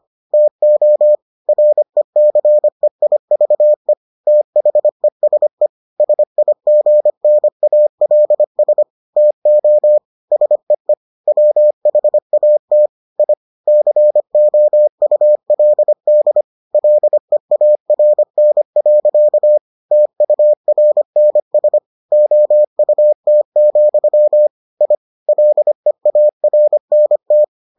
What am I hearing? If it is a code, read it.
Morse code, heard as a keyed tone, about 25 wpm, 600 Hz. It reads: TO RECEIVE THESE SIGNALS TO SEE WHAT I COULD LEARN. TURNS OUT, I LEARNT